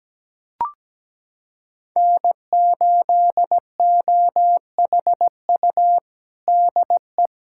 Morse code, heard as a keyed tone, about 17 wpm, 700 Hz. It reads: N8OHU DE
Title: USA Callsign DE Callsign - 5 Characters - 17wpm